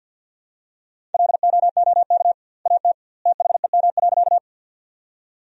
Morse code, heard as a keyed tone, about 52 wpm, 700 Hz. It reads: LOOK AT THEM.
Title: 2x Word-Spacing - Sentences from Top 100 Words - 50wpm